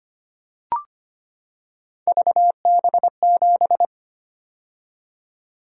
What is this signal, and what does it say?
Morse code, heard as a keyed tone, about 25 wpm, 700 Hz. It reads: VB7
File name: Three Letter-Number Combinations - 25wpm